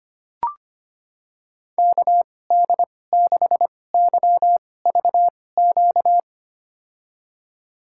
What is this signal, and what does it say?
Morse code, heard as a keyed tone, about 25 wpm, spaced as for 18 wpm, 700 Hz. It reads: KD6YVQ
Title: USA Call Signs - 6 Characters - 2x Character Spacing - 25wpm